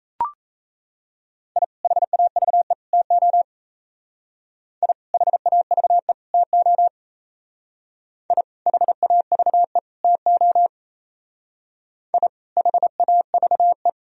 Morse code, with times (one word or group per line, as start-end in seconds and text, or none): 1.6-1.6 I
1.8-2.7 HAVE
2.9-3.4 TO
4.8-4.9 I
5.1-6.1 HAVE
6.3-6.9 TO
8.3-8.4 I
8.7-9.8 HAVE
10.0-10.7 TO
12.1-12.3 I
12.6-13.9 HAVE